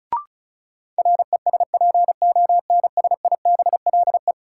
Morse code, heard as a keyed tone, about 34 wpm, 700 Hz. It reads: RESPONSIBLE
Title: Sentences from Top 1000 Words - Review Part 7 - New Word Drill - 35wpm